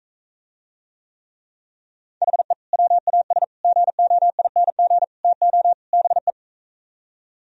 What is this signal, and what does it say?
Morse code, heard as a keyed tone, about 45 wpm, 700 Hz. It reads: HE WAS GOING TO BE